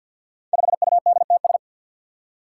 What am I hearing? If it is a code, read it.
Morse code, heard as a keyed tone, about 52 wpm, 700 Hz. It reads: 5UDTS
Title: Five Challenging Letter-Number Combinations 50wpm